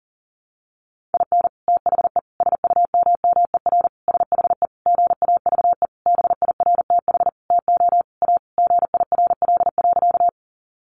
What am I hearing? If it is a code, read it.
Morse code, heard as a keyed tone, about 37 wpm, 700 Hz. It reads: IN THE SUMMER SHE GAVE BIRTH TO A GIRL.
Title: Sentences from Top 1100 Words 40wpm